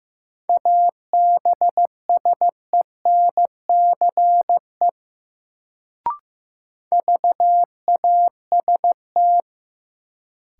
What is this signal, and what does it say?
Morse code, heard as a keyed tone, about 15 wpm, 700 Hz. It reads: ABSENCE VAST
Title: Sets of 4 Words - New Word Drill - 15wpm